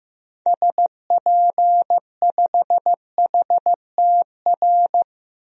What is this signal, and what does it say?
Morse code, heard as a keyed tone, about 15 wpm, 700 Hz. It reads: SP5HTR
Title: DXCC-100 Call Signs - 6 Characters - 15wpm